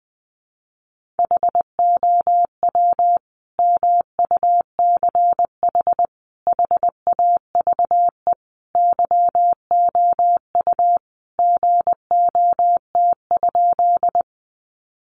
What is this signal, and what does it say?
Morse code, heard as a keyed tone, about 20 wpm, 700 Hz. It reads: HOW MUCH HAVE YOU GOT?
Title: Sentences from Top 300 Words 20wpm